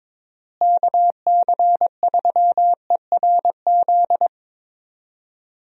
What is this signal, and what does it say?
Morse code, heard as a keyed tone, about 22 wpm, 700 Hz. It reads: KC3ERZ
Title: USA Call Signs - 6 Characters - Repeated 3x - 22wpm